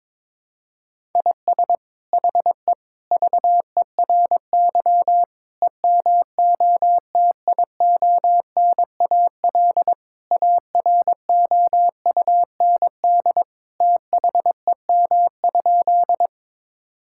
Morse code, heard as a keyed tone, about 22 wpm, 700 Hz. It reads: IS HE VERY EMOTIONAL AROUND THEM?
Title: Sentences from Top 1000 Words - Review Part 5 - 22wpm